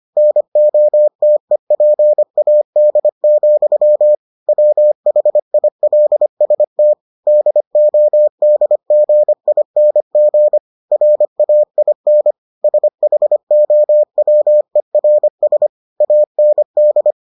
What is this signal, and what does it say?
Morse code, heard as a keyed tone, about 25 wpm, 600 Hz. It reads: NOTEPAD, WHILST DODGING RAIN SHOWERS AND